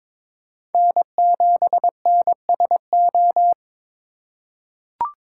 Morse code, heard as a keyed tone, about 22 wpm, 700 Hz. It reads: N7NSO
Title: USA Call Signs - 5 Characters - 3x Character Spacing - 22wpm